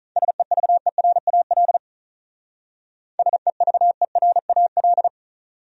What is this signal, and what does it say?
Morse code, heard as a keyed tone, about 43 wpm, 700 Hz. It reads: SEVERAL SEVERAL
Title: Speed-Racing Top 500 Words 35wpm